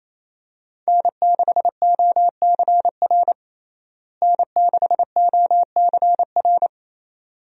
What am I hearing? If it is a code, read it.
Morse code, heard as a keyed tone, about 28 wpm, 700 Hz. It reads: N6OCR N6OCR
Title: USA Call Signs - 5 Characters - Repeated 2x - 28wpm